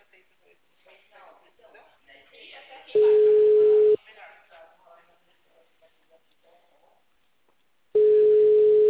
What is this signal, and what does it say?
Ringback tone (Europe, 425 Hz), with speech.
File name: out- -2003- - - .84078